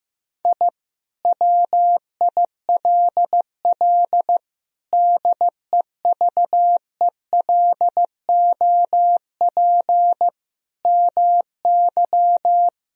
Morse code, heard as a keyed tone, about 15 wpm, 700 Hz. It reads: I WILL DEVELOP MY
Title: Sentences from Top 1000 Words - Review Part 2 - 15wpm